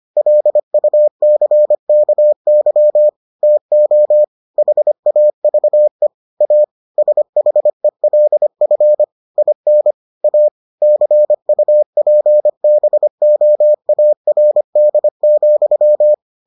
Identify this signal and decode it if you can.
Morse code, heard as a keyed tone, about 25 wpm, 600 Hz. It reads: LUCKY TO HAVE A SHELF IN A CUPBOARD,